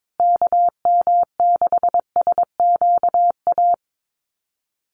Morse code, heard as a keyed tone, about 22 wpm, 700 Hz. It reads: KM6SQA
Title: USA Call Signs Repeated 2x 22wpm